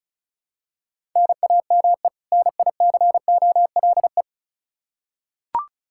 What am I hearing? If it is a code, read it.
Morse code, heard as a keyed tone, about 36 wpm, 700 Hz. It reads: NAME NICOLE
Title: Practice QSO - Exchange 1 - Beginning - 35wpm